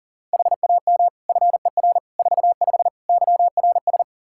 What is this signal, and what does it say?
Morse code, heard as a keyed tone, about 40 wpm, 700 Hz. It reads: HAM FER 45 YRS
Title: K1USN SST QSO Practice 40wpm